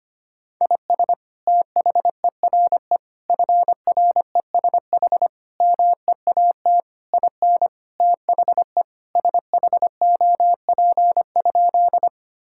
Morse code, heard as a keyed tone, about 25 wpm, 700 Hz. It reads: IS THERE FRESH MEAT IN THE SHOP?
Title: Sentences from Top 1600 Words 25wpm